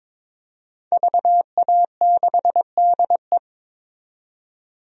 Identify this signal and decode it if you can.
Morse code, heard as a keyed tone, about 22 wpm, 700 Hz. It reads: VA6DE
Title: DXCC-100 Call Signs - 5 Characters - 22wpm